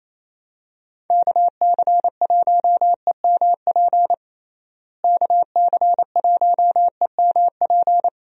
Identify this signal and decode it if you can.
Morse code, heard as a keyed tone, about 28 wpm, 700 Hz. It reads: KC1EMP KC1EMP
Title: USA Call Signs - 6 Characters - Repeated 2x - 28wpm